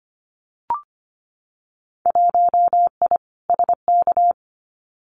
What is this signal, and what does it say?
Morse code, heard as a keyed tone, about 25 wpm, 700 Hz. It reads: 1I SK
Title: Winter Field Day Exchanges Encore 25wpm